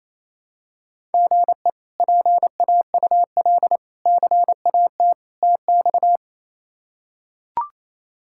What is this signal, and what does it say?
Morse code, heard as a keyed tone, about 28 wpm, 700 Hz. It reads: GE PAUL CAT TX
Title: K1USN SST QSO Practice - 28wpm